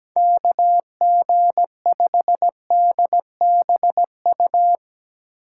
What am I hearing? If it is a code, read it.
Morse code, heard as a keyed tone, about 17 wpm, 700 Hz. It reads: KG5DBU
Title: USA Call Signs - 6 Characters - 3x Character Spacing - 17wpm